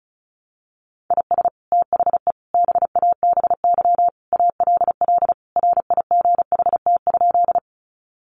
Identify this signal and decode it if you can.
Morse code, heard as a keyed tone, about 34 wpm, 700 Hz. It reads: IS THE BABY ALL RIGHT?
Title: Sentences from Top 1300 Words 35wpm